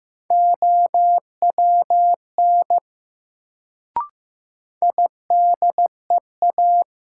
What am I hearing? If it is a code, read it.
Morse code, heard as a keyed tone, about 15 wpm, 700 Hz. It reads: OWN IDEA
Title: Sentences from Top 300 Words - New Word Drill - 15wpm